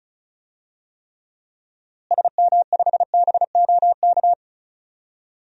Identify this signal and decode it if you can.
Morse code, heard as a keyed tone, about 36 wpm, 700 Hz. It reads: SM5BOK
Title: DXCC-100 Call Signs - 6 Characters - Repeated 3x - 35wpm